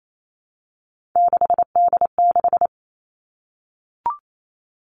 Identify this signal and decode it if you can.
Morse code, heard as a keyed tone, about 28 wpm, 700 Hz. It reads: 6D6
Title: All Cut Numbers - 3-Digit 28wpm